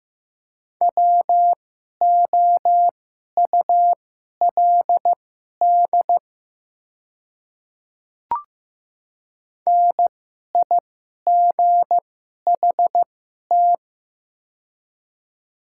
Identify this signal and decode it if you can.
Morse code, heard as a keyed tone, about 15 wpm, spaced as for 11 wpm, 700 Hz. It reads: WOULD NIGHT
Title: 2x Character Spacing - Top 300 Words - 15wpm